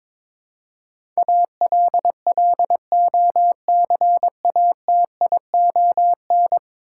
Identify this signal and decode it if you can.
Morse code, heard as a keyed tone, about 21 wpm, 700 Hz. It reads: ALLOCATION
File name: practice / 22wpm / 10 Letter Words - First 3 Letters Given - 22wpm